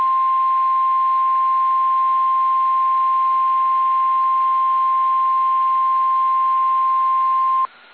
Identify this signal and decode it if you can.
Emergency Alert System (SAME) tones (the 1050 Hz attention tone). It is heard from 1.1 to 7.0 s.